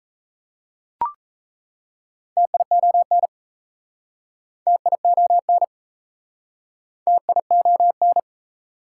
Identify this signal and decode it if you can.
Morse code, heard as a keyed tone, about 38 wpm, 690 Hz. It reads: TION TION TION